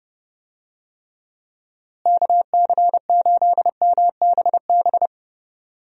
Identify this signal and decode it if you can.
Morse code, heard as a keyed tone, about 30 wpm, 700 Hz. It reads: KC8MBB
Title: USA Call Signs - 6 Characters - Repeated 2x - 30wpm